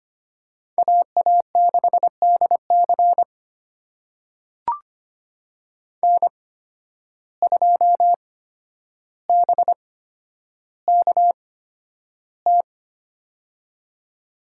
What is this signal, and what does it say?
Morse code, heard as a keyed tone, about 25 wpm, spaced as for 11 wpm, 700 Hz. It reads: AA6DC N2BKT
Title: USA Call Signs - 5 Characters - 8x Character Spacing - 25wpm